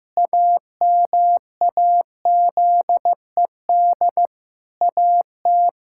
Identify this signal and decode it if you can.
Morse code, heard as a keyed tone, about 15 wpm, 700 Hz. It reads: AMAZED AT
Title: Pangram Challenge v2 - 15wpm